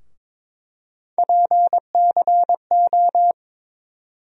Morse code, heard as a keyed tone, about 22 wpm, 700 Hz. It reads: PCO
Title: USA Call Sign Postfixes - 3 Letters 22wpm